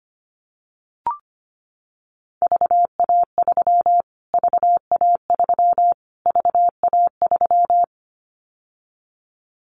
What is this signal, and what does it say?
Morse code, heard as a keyed tone, about 25 wpm, 700 Hz. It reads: VA3 VA3 VA3
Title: Callsign Prefixes for Top 25 DXCC - Repeated 3x - 25wpm